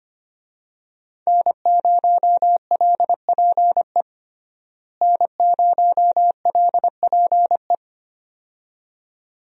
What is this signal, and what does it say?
Morse code, heard as a keyed tone, about 25 wpm, 700 Hz. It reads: N0LPE N0LPE
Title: USA Call Signs - 5 Characters - Repeated 2x - 25wpm